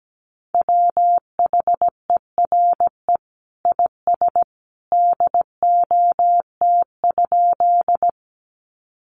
Morse code, heard as a keyed tone, about 17 wpm, 700 Hz. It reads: WHERE IS DOT?
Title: Sentences from Top 1500 Words 17wpm